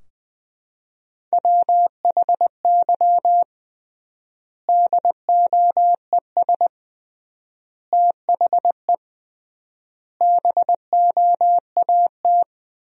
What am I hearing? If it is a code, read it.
Morse code, heard as a keyed tone, about 21 wpm, 700 Hz. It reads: WHY DOES THE BOAT
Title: 3x Word-Spacing - Sentences from Top 500 Words - 20wpm